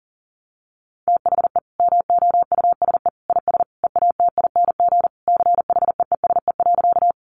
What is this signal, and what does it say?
Morse code, heard as a keyed tone, about 40 wpm, 700 Hz. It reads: THE MOUSE IS EATING CHEESE.
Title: Easy Phrases 40wpm